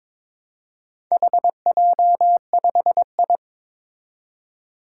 Morse code, heard as a keyed tone, about 22 wpm, 700 Hz. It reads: HJ5I
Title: Four Challenging Letter-Number Combinations 22wpm